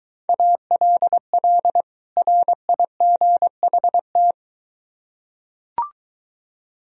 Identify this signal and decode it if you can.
Morse code, heard as a keyed tone, about 22 wpm, 700 Hz. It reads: ALL RIGHT